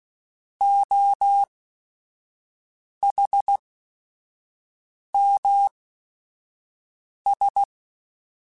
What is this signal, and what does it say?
Morse code, heard as a keyed tone, about 16 wpm, spaced as for 5 wpm, 800 Hz. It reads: OHMS
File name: Technical Subjects / Technical Subjects-mp3s / XL @5